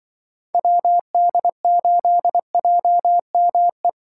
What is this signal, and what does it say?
Morse code, heard as a keyed tone, about 24 wpm, 700 Hz. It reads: WD8JME
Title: Speed Racing - USA 2x3 Call Signs 20wpm